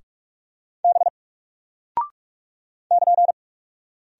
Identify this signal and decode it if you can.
Morse code, heard as a keyed tone, about 48 wpm, 700 Hz. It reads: B <KN>
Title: Single Character (Letter:Number:Punctuation:Prosign) Alt 45wpm